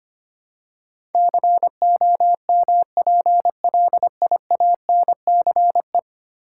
Morse code, heard as a keyed tone, about 25 wpm, 700 Hz. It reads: COMPLIANCE
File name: 10 Letter Words - First 3 Letters Given - 25wpm